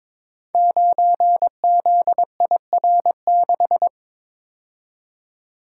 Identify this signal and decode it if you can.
Morse code, heard as a keyed tone, about 22 wpm, 700 Hz. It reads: 9ZIR6
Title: Code Groups - 5 Letter:Numbers - 22wpm